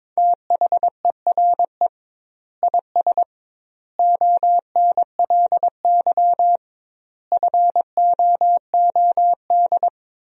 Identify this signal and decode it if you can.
Morse code, heard as a keyed tone, about 22 wpm, 700 Hz. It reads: THERE IS ONLY FOOD
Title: 2x Word-Spacing - Sentences from Top 300 Words - 22wpm